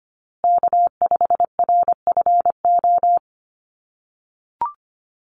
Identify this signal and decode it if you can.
Morse code, heard as a keyed tone, about 25 wpm, 700 Hz. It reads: K5RFO
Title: USA Call Signs Repeated 2x 25wpm